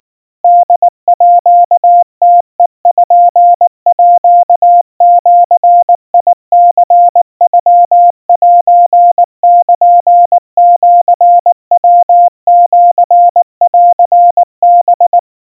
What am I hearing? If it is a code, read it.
Morse code (Japanese Wabun code), heard as a keyed tone, about 19 wpm, 700 Hz. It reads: ホームページニノセルシヤシンハ
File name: kawa / Morse sample / kamyu 70 cpm 700 Hz Crispy